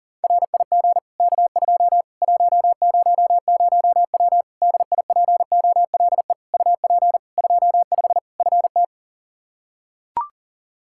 Morse code, heard as a keyed tone, about 40 wpm, 700 Hz. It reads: RIG K2 100W DIPOLE UP 25 FT
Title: Practice QSO - Exchange 2 - Info - 40wpm